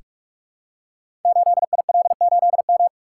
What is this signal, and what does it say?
Morse code, heard as a keyed tone, about 44 wpm, 700 Hz. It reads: 8IL8M
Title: Code Groups - 5 Letter:Numbers - 45wpm